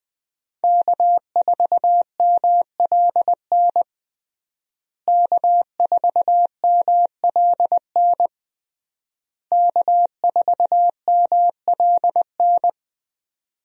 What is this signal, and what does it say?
Morse code, heard as a keyed tone, about 21 wpm, 700 Hz. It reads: K4MLN K4MLN K4MLN
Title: USA Call Signs - 5 Characters - Repeated 3x - 20wpm